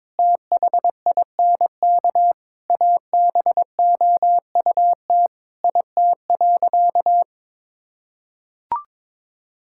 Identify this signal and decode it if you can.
Morse code, heard as a keyed tone, about 22 wpm, 700 Hz. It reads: THINK ABOUT IT.
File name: Sentences with 7-Word Phrases - 22wpm